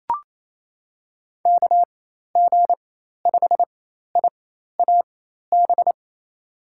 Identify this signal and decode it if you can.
Morse code, heard as a keyed tone, about 28 wpm, spaced as for 13 wpm, 700 Hz. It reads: KG5IAB
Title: USA Call Signs - 6 Characters - 4x Character Spacing - 28wpm